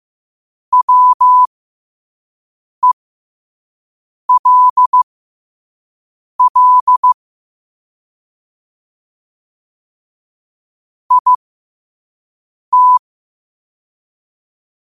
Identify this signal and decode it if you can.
Morse code, heard as a keyed tone, about 15 wpm, spaced as for 5 wpm, 1000 Hz. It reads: WELL IT
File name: qotdfiles / 5wpm